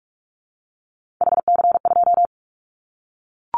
Morse code, heard as a keyed tone, about 44 wpm, 700 Hz. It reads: HC2
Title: Callsign Prefixes for Top 100 DXCC - 45wpm